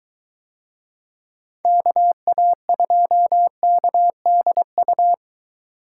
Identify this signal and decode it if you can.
Morse code, heard as a keyed tone, about 21 wpm, 700 Hz. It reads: KA2KDU